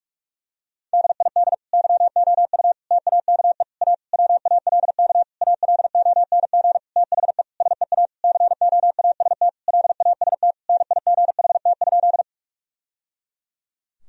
Morse code, heard as a keyed tone, about 46 wpm, 700 Hz. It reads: DID YOU TAKE A WALK ALONG THE SEA COAST LAST NIGHT?
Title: Sentences from Top 1000 Words - Review Part 4 - 45wpm